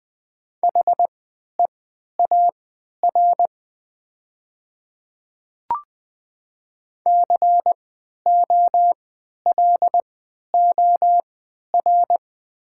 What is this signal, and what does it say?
Morse code, heard as a keyed tone, about 20 wpm, spaced as for 11 wpm, 700 Hz. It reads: HEAR COLOR
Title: 3x Character Spacing - Top 300 Words - 20wpm